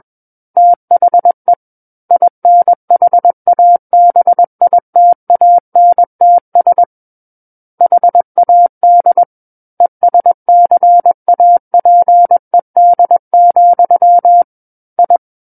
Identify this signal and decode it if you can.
Morse code, heard as a keyed tone, about 21 wpm, 700 Hz. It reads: THE INHABITANTS HAD ESCAPED, I
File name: Book One / 12-Chapter 12 - 21 WPM